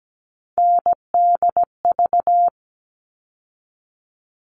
Morse code, heard as a keyed tone, about 17 wpm, 700 Hz. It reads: NDV
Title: All Cut Numbers - 3-Digit 17wpm